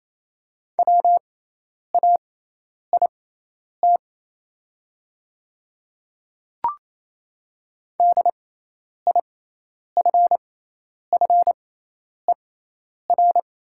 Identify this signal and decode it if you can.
Morse code, heard as a keyed tone, about 27 wpm, spaced as for 10 wpm, 690 Hz. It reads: WAIT DIFFER
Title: 6x Character Spacing - Top 500 Words - 28wpm